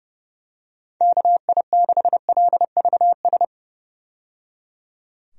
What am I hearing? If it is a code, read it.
Morse code, heard as a keyed tone, about 30 wpm, 700 Hz. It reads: KI6LVS
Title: USA Call Signs - 6 Characters - 30wpm